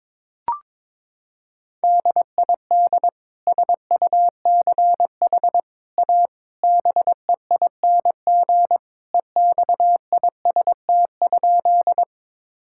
Morse code, heard as a keyed tone, about 22 wpm, 700 Hz. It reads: DID SUCH A BEING EXIST?
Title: Sentences from Top 1000 Words - Review Part 5 - 22wpm